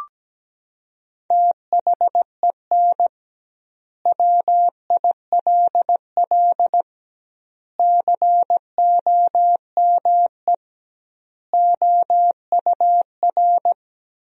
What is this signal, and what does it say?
Morse code, heard as a keyed tone, about 17 wpm, 700 Hz. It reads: THEN WILL COME OUR
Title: 2x Word-Spacing - Sentences from Top 200 Words - 17wpm